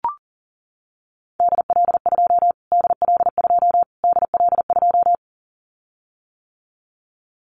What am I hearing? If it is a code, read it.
Morse code, heard as a keyed tone, about 40 wpm, 700 Hz. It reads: DL2 DL2 DL2
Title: Callsign Prefixes for Top 25 DXCC - Repeated 3x - 40wpm